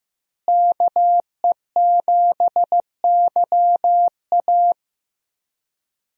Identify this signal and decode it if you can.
Morse code, heard as a keyed tone, about 15 wpm, 700 Hz. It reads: KE7YA
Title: USA Call Signs - 5 Characters - Repeated 3x - 15wpm